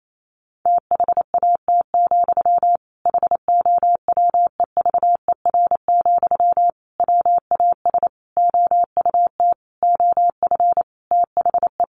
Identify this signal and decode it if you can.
Morse code, heard as a keyed tone, about 28 wpm, 700 Hz. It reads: THAT, HOWEVER, WAS OUT OF THE
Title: Sentences from Top 600 Words 28wpm